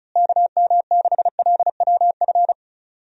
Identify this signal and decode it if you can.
Morse code, heard as a keyed tone, about 36 wpm, 700 Hz. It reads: KM6LWF
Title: USA Call Signs - 6 Characters - Repeated 2x - 35wpm